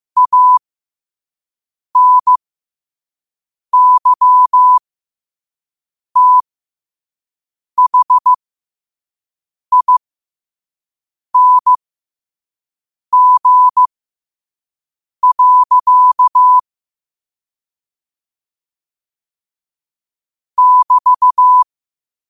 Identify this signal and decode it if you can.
Morse code, heard as a keyed tone, about 15 wpm, spaced as for 5 wpm, 1000 Hz. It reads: ANYTHING. <BT>
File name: qotdfiles / 5wpm